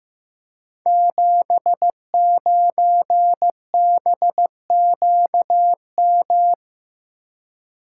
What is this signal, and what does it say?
Morse code, heard as a keyed tone, about 15 wpm, 700 Hz. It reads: 79BQM
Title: Code Groups - 5 Letter:Numbers - 15wpm